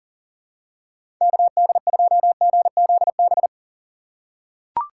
Morse code, heard as a keyed tone, about 40 wpm, 700 Hz. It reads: KD2GZB
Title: USA Call Signs - 6 Characters - 3x Character Spacing - 40wpm